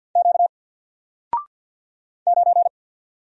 Morse code, heard as a keyed tone, about 48 wpm, 700 Hz. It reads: Q 9
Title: Single Letter-Number 50wpm